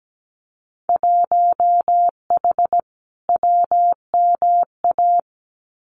Morse code, heard as a keyed tone, about 17 wpm, 700 Hz. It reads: 1H WMA
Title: Winter Field Day Exchanges Encore 17wpm